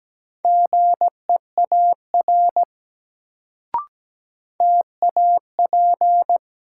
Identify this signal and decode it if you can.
Morse code, heard as a keyed tone, about 17 wpm, 700 Hz. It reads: GEAR TAP
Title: Sentences from Top 1400 Words - New Word Drill Alt - 17wpm